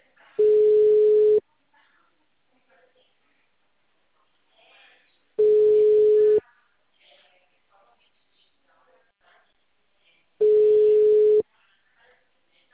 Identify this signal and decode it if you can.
Ringback tone (Europe, 425 Hz), with speech.
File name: out- -2055- - - .35823